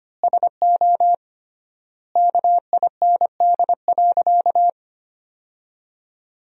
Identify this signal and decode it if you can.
Morse code, heard as a keyed tone, about 25 wpm, 700 Hz. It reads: SO KIND.
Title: 3x Word-Spacing - Sentences from Top 500 Words - 25wpm